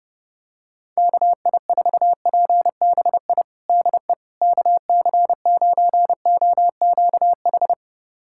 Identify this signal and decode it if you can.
Morse code, heard as a keyed tone, about 30 wpm, 700 Hz. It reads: KI4PBI DE KC9OQH
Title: USA Callsign DE Callsign 30wpm